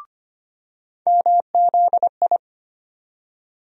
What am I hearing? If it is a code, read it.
Morse code, heard as a keyed tone, about 25 wpm, 700 Hz. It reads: MZI
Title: USA Call Sign Postfixes - 3 Letters 25wpm